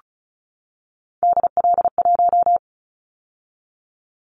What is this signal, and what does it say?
Morse code, heard as a keyed tone, about 36 wpm, 700 Hz. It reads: DL1
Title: Callsign Prefixes for Top 25 DXCC - 35wpm